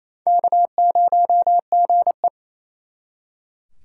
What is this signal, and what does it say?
Morse code, heard as a keyed tone, about 37 wpm, 700 Hz. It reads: K0GE